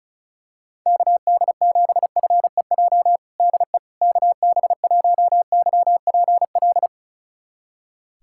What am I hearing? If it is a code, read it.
Morse code, heard as a keyed tone, about 34 wpm, 700 Hz. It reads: KD7FEJ DE KB1YPL
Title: USA Callsign DE Callsign - 6 Characters - 35wpm